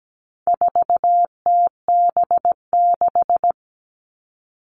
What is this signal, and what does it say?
Morse code, heard as a keyed tone, about 17 wpm, 700 Hz. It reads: 4TB6
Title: All Cut Numbers - 4-Digit 17wpm